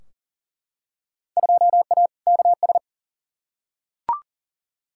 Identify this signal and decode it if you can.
Morse code, heard as a keyed tone, about 40 wpm, 700 Hz. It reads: 2A KS
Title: Field Day Exchanges - 40wpm